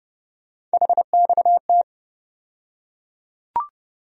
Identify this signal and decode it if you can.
Morse code, heard as a keyed tone, about 30 wpm, 700 Hz. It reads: HXT
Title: USA Call Sign Postfixes - 3 Letters 30wpm